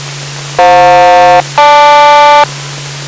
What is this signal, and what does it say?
Aircraft SELCAL, heard as two pairs of tones. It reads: FJ-GL